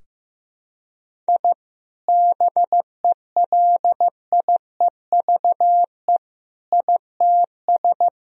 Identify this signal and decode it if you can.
Morse code, heard as a keyed tone, about 15 wpm, 700 Hz. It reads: I BELIEVE ITS
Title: Sentences from Top 1000 Words - Review Part 3 - 15wpm